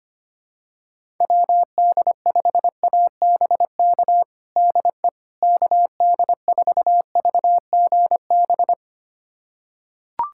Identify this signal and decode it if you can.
Morse code, heard as a keyed tone, about 25 wpm, 700 Hz. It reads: WD5ABK DE KD4VGB
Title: USA Callsign DE Callsign 25wpm